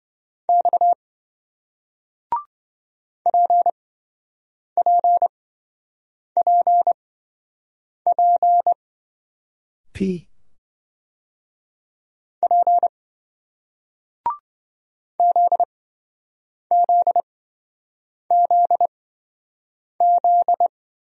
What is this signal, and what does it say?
Morse code, heard as a keyed tone, about 27 wpm, spaced as for 7 wpm, 700 Hz. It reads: X PPPP P ZZZZ